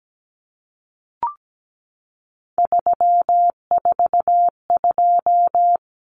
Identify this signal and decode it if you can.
Morse code, heard as a keyed tone, about 17 wpm, 700 Hz. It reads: 342
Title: Numbers - 3-Digit 17wpm